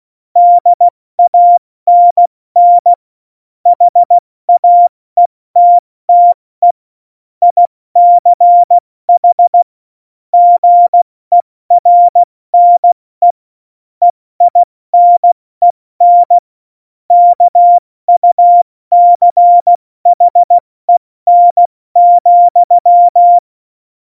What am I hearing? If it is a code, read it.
Morse code, heard as a keyed tone, about 16 wpm, 700 Hz. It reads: DANN HAETTE ICH GERNE EINEN KUCHEN,
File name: Morsekurs / Witze / Witz 23. April 2019, 70 F80